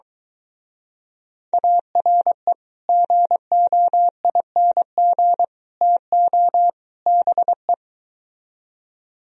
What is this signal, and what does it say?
Morse code, heard as a keyed tone, about 22 wpm, 700 Hz. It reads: ARE GOING TO BE